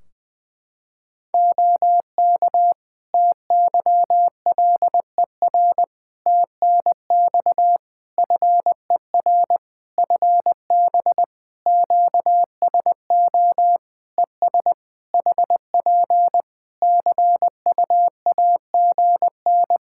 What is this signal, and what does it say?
Morse code, heard as a keyed tone, about 20 wpm, 700 Hz. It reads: OK TYLER TNX FER FB QSO ES HP CUAGN
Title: Practice QSO - Exchange 4 - Ending - 20wpm